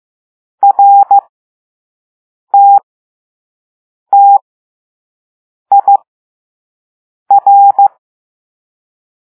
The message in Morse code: RTTIR